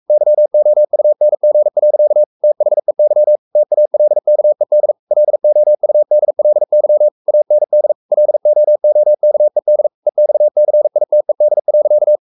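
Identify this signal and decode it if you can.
Morse code, heard as a keyed tone, about 44 wpm, 600 Hz. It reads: YOUNG. THEY TALKED LOUDLY AND LOOKED EXCITED.
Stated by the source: Morse code at 43 wpm sending Ken Follett's novel Never in English, from CW med Gnister.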